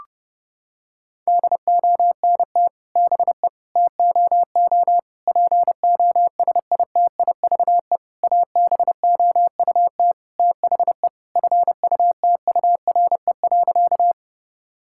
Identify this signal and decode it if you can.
Morse code, heard as a keyed tone, about 30 wpm, 700 Hz. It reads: DONT BE TOO POSITIVE ABOUT THE FUTURE.
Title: Sentences from Top 1000 Words - Review Part 5 - 30wpm